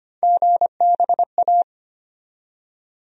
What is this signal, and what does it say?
Morse code, heard as a keyed tone, about 25 wpm, 700 Hz. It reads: GBA
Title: USA Call Sign Postfixes - 3 Letters 25wpm